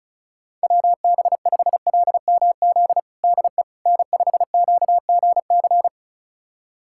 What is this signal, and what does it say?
Morse code, heard as a keyed tone, about 36 wpm, 700 Hz. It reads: WB5LMZ DE N5QGC
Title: USA Callsign DE Callsign 35wpm